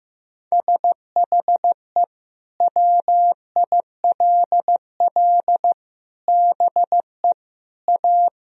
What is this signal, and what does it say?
Morse code, heard as a keyed tone, about 16 wpm, 700 Hz. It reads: SHE WILL BE A
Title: Sentences from Top 1000 Words - Review Part 4 - 15wpm